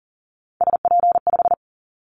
Morse code, heard as a keyed tone, about 38 wpm, 700 Hz. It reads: SP5
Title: Callsign Prefixes for Top 25 DXCC - 40wpm